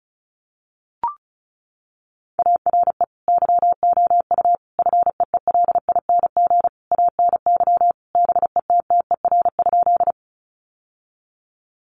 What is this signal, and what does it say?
Morse code, heard as a keyed tone, about 34 wpm, 700 Hz. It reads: ARE YOU FEELING ANY BETTER?
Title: Sentences from Top 1200 Words 35wpm